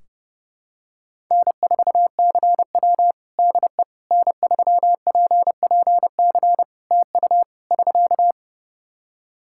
Morse code, heard as a keyed tone, about 30 wpm, 700 Hz. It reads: N4CW DE N3PPC TU <SK>
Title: Practice QSO - Exchange 4 - Ending - 30wpm